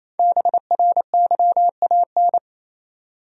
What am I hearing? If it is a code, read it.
Morse code, heard as a keyed tone, about 28 wpm, 700 Hz. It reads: BRYAN
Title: USA Common Names - 28wpm